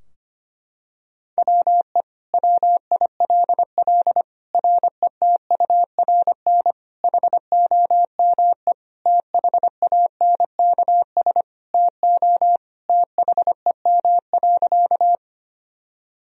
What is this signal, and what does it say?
Morse code, heard as a keyed tone, about 25 wpm, 700 Hz. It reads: WE WILL RETURN HOME THANKS TO THEM.
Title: Sentences from Top 1000 Words - Review Part 3 - 25wpm